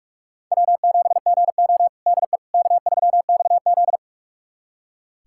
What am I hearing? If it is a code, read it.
Morse code, heard as a keyed tone, about 46 wpm, 700 Hz. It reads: W7GO DE K3XZ
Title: USA Callsign DE Callsign - 4 Characters - 45wpm